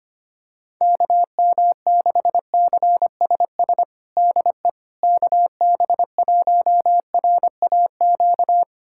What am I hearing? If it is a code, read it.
Morse code, heard as a keyed tone, about 25 wpm, 700 Hz. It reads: KM6CSS DE KB1RAQ
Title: USA Callsign DE Callsign 25wpm